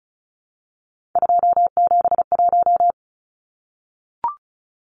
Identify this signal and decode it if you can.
Morse code, heard as a keyed tone, about 34 wpm, 700 Hz. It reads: 271
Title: Numbers - 3-Digit 35wpm